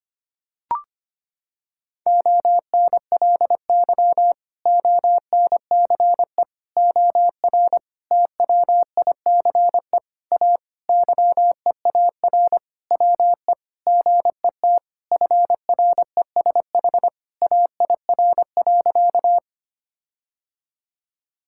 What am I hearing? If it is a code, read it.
Morse code, heard as a keyed tone, about 25 wpm, 700 Hz. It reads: ONLY ONCE OR TWICE A YEAR WE GET FRESH AIR.
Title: Sentences from Top 1600 Words 25wpm